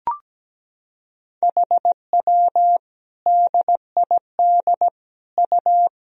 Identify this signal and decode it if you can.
Morse code, heard as a keyed tone, about 17 wpm, 700 Hz. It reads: HW DID U
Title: Sentences from Top 100 Words and Abbreviations 17wpm